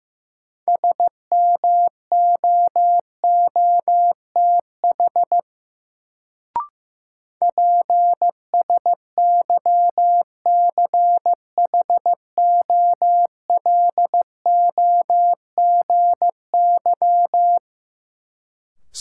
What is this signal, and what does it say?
Morse code, heard as a keyed tone, about 15 wpm, 700 Hz. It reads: SMOOTH PSYCHOLOGY
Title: Sentences from Top 1400 Words - New Word Drill Alt - 15wpm